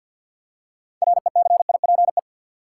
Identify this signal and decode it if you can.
Morse code, heard as a keyed tone, about 53 wpm, 700 Hz. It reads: RECIPE
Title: Six Letter Words - 50wpm